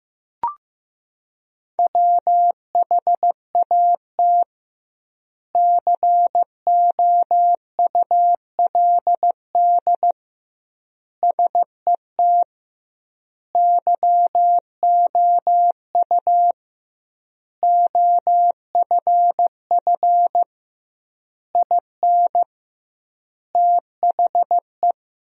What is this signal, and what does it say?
Morse code, heard as a keyed tone, about 15 wpm, 700 Hz. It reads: WHAT COULD SET YOU OFF IN THE
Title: 2x Word-Spacing - Sentences from Top 200 Words - 15wpm